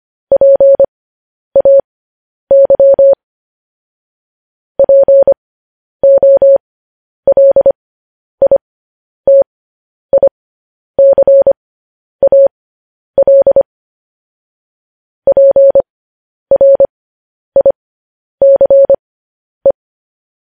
Morse code, heard as a keyed tone, about 25 wpm, spaced as for 10 wpm, 550 Hz. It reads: PAY POLITICAL PRICE